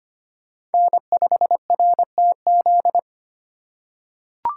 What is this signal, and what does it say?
Morse code, heard as a keyed tone, about 25 wpm, 700 Hz. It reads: N5RTZ
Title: USA Call Signs - 2x Character Spacing - 25wpm